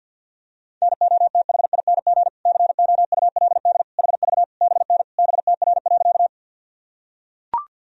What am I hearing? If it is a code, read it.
Morse code, heard as a keyed tone, about 45 wpm, 700 Hz. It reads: NOTHING COULD HV BN BTR.
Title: Sentences from Top 500 Words and Abbreviations 50wpm